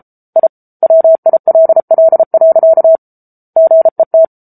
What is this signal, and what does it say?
Morse code, heard as a keyed tone, about 34 wpm, 650 Hz. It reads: I WILL. GET
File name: Book Two / 07-Chapter 7 - 33 WPM